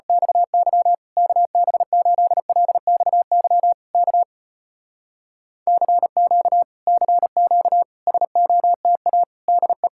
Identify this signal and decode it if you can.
Morse code, heard as a keyed tone, about 34 wpm, 700 Hz. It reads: XY KB8LXY K CQ CQ SOTA DE